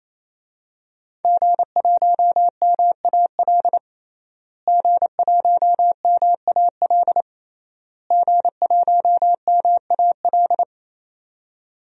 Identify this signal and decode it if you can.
Morse code, heard as a keyed tone, about 28 wpm, 700 Hz. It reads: G1MAL G1MAL G1MAL
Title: DXCC-100 Call Signs - 5 Characters - Repeated 3x - 28wpm